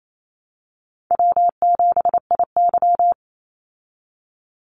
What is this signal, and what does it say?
Morse code, heard as a keyed tone, about 28 wpm, 700 Hz. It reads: W7IY
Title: Call Signs Repeated 2x - 28wpm